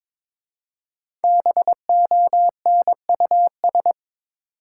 Morse code, heard as a keyed tone, about 22 wpm, 700 Hz. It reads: BONUS
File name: 5 Letter Words - First 3 Letters Given - 22wpm